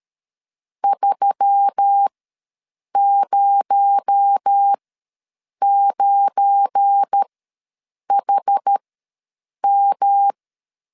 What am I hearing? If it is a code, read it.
Morse code, heard as a keyed tone, about 13 wpm, spaced as for 7 wpm, 800 Hz. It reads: 309HM
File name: LEZIONI / 7 GF9Z / 7 GF9 ws 12wpm 3344